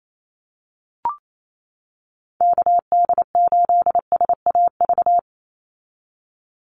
Morse code, heard as a keyed tone, about 28 wpm, 700 Hz. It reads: KD8SAV
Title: Call Signs - 28wpm